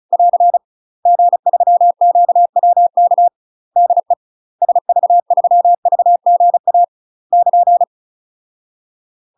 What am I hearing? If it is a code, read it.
Morse code, heard as a keyed tone, about 35 wpm, 700 Hz. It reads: <AR> G3QWK DE SV3VGA <KN>
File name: UK MASTER QSOs TRAINING / 35WPM / UK MASTER QSOs 0013